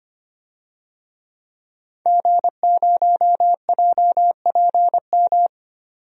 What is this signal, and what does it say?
Morse code, heard as a keyed tone, about 25 wpm, 700 Hz. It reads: G0JPM
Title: DXCC-100 Call Signs - 5 Characters - 25wpm